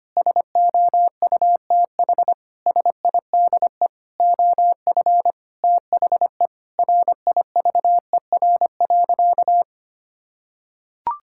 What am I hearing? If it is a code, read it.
Morse code, heard as a keyed tone, about 25 wpm, 700 Hz. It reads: SOUTH SIDE OF THE RIVER.
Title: Sentences from Top 1000 Words - Review Part 5 - 25wpm